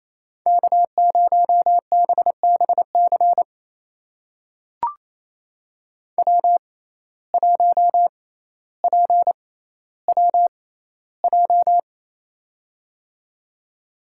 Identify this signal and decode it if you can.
Morse code, heard as a keyed tone, about 28 wpm, spaced as for 14 wpm, 700 Hz. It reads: K0BBC W1PWJ
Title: USA Call Signs - 5 Characters - 6x Character Spacing - 28wpm